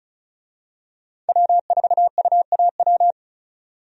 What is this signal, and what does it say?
Morse code, heard as a keyed tone, about 36 wpm, 700 Hz. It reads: W4UAW
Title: USA Call Signs - 5 Characters - 35wpm